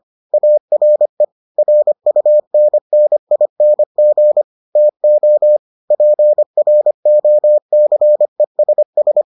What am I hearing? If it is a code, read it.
Morse code, heard as a keyed tone, about 25 wpm, 600 Hz. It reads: ARE RUNNING TO PROCESS